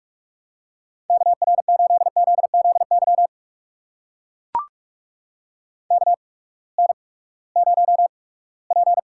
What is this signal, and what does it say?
Morse code, heard as a keyed tone, about 44 wpm, spaced as for 22 wpm, 700 Hz. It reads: KR8ZZY KN0P
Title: USA Call Signs - 8x Character Spacing - 45wpm